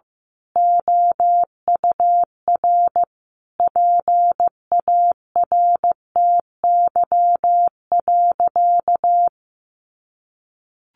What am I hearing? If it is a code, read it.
Morse code, heard as a keyed tone, about 15 wpm, 700 Hz. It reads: OUR PARTY.
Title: Sentences from Top 1400 Words 15wpm